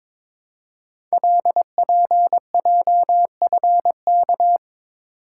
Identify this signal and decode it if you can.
Morse code, heard as a keyed tone, about 22 wpm, 700 Hz. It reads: LPJFK
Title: Code Groups - 5 Letters - 22wpm